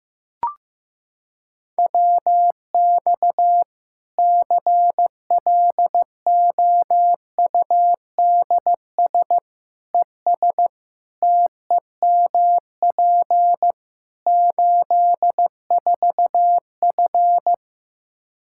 Morse code, heard as a keyed tone, about 15 wpm, 700 Hz. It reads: WX CLOUDS ES TEMP 84F
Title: Practice QSO - Exchange 2 - Info - 15wpm